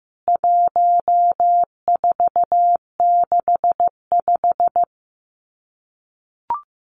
Morse code, heard as a keyed tone, about 15 wpm, 700 Hz. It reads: 1465
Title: CWT Exchange 15wpm